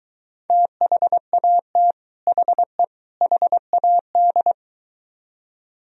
Morse code, heard as a keyed tone, about 22 wpm, 700 Hz. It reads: THAT HE HAD